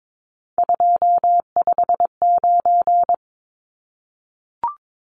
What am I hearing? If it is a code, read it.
Morse code, heard as a keyed tone, about 22 wpm, 700 Hz. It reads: 259
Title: Numbers - 3-Digit 22wpm